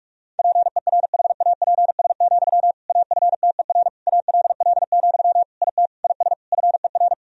Morse code, heard as a keyed tone, about 46 wpm, 690 Hz. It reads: PERHAPS, AFTER ALL, IT IS FER
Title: Sentences from Top 500 Words and Abbreviations 45wpm